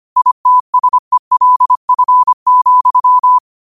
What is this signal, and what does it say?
Morse code, heard as a keyed tone, about 25 wpm, 1000 Hz. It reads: ITSELF,